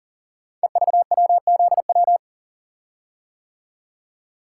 Morse code, heard as a keyed tone, about 40 wpm, 700 Hz. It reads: EVWZW
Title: Code Groups - 5 Letter:Numbers - 40wpm